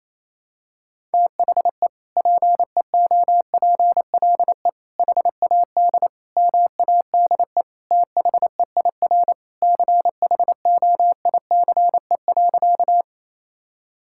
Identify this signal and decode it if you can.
Morse code, heard as a keyed tone, about 27 wpm, 700 Hz. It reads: THE PEOPLE HAD MADE THEIR CHOICE.
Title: Sentences from Top 1000 Words - Review Part 6 - 28wpm